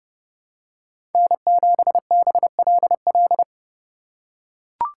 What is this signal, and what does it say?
Morse code, heard as a keyed tone, about 30 wpm, 700 Hz. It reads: N7BLL
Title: USA Call Signs - 3x Character Spacing - 30wpm